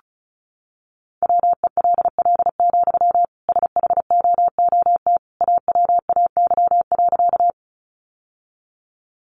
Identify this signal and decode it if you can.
Morse code, heard as a keyed tone, about 34 wpm, 700 Hz. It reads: WELL, SHOOT AWAY.
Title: Sentences from Top 1200 Words 35wpm